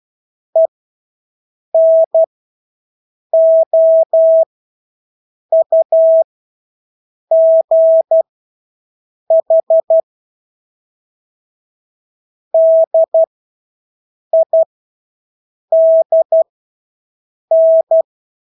Morse code, heard as a keyed tone, about 12 wpm, spaced as for 6 wpm, 650 Hz. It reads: ENOUGH DIDN
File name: Engelske ord 12-6 wpm